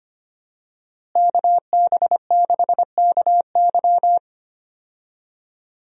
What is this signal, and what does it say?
Morse code, heard as a keyed tone, about 25 wpm, 700 Hz. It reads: KB6KY
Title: USA Call Signs - 5 Characters - Repeated 3x - 25wpm